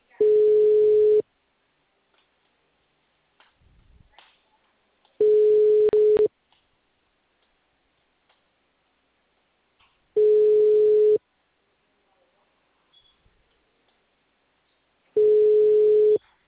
Ringback tone (Europe, 425 Hz).
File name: out- -2003- - - .60293